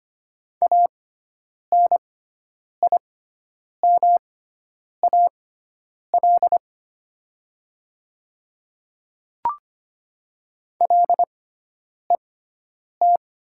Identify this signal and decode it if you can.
Morse code, heard as a keyed tone, about 25 wpm, spaced as for 9 wpm, 700 Hz. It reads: ANIMAL LET
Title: 6x Character Spacing - Top 300 Words - 25wpm